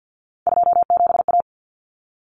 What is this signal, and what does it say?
Morse code, heard as a keyed tone, about 50 wpm, 700 Hz. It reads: 27A
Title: Common Cut Numbers - 3-Digit 50wpm